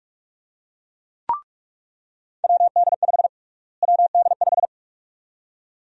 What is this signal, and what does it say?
Morse code, heard as a keyed tone, about 46 wpm, 700 Hz. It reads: WD5 WD5
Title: USA Call Sign Prefixes - Repeated 2x - 45wpm